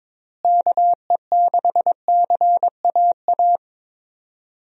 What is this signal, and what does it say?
Morse code, heard as a keyed tone, about 22 wpm, 700 Hz. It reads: KE6CAA
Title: USA Call Signs - 6 Characters - Repeated 3x - 22wpm